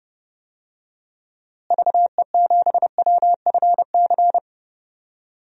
Morse code, heard as a keyed tone, about 30 wpm, 700 Hz. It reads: VE7WFC
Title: DXCC-100 Call Signs - 6 Characters - Repeated 3x - 30wpm